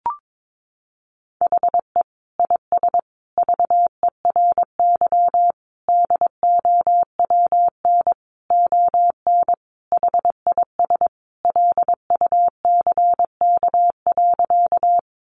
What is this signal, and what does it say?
Morse code, heard as a keyed tone, about 22 wpm, 700 Hz. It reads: HE IS VERY DOWN ON HIS LUCK.
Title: Sentences from Top 1500 Words 22wpm